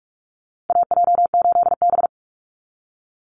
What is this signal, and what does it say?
Morse code, heard as a keyed tone, about 44 wpm, 700 Hz. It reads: AJ8B
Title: Call Signs Repeated 2x - 45wpm